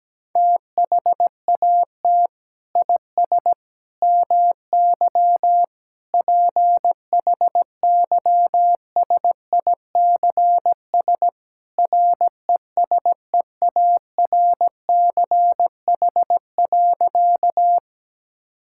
Morse code, heard as a keyed tone, about 17 wpm, 700 Hz. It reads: THAT IS MY PHYSICS RESEARCH.
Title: Sentences from Top 1000 Words - Review Part 5 - 17wpm